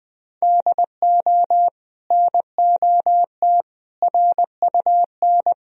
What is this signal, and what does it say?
Morse code, heard as a keyed tone, about 20 wpm, 700 Hz. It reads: DO NOT RUN
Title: Sentences from Top 1000 Words - Review Part 1 - 20wpm